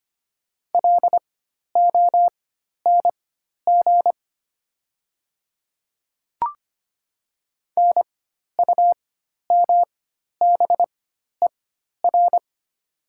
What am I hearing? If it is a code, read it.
Morse code, heard as a keyed tone, about 25 wpm, spaced as for 12 wpm, 700 Hz. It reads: LONG NUMBER
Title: 4x Character Spacing - Top 100 Words - 25wpm